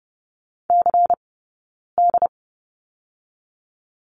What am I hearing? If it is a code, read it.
Morse code, heard as a keyed tone, about 30 wpm, 700 Hz. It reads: C D
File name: Lesson 19d - Letter w - Two Characters 30wpm